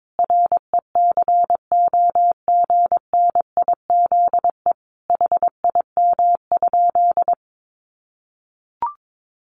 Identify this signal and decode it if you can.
Morse code, heard as a keyed tone, about 22 wpm, 700 Hz. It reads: RECOGNIZE HIM?
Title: Sentences from Top 1400 Words 22wpm